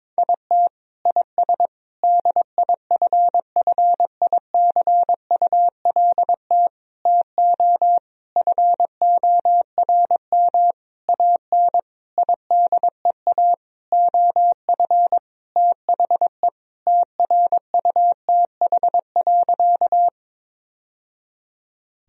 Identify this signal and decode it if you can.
Morse code, heard as a keyed tone, about 22 wpm, 700 Hz. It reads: IT IS DIFFICULT TO FORM AN IDEA OF THE TRUTH.
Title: Sentences from Top 1000 Words - Review Part 1 - 22wpm